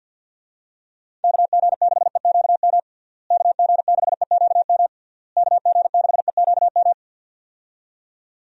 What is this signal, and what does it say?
Morse code, heard as a keyed tone, about 48 wpm, 700 Hz. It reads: KG6EQM KG6EQM KG6EQM
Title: USA Call Signs - 6 Characters - Repeated 3x - 50wpm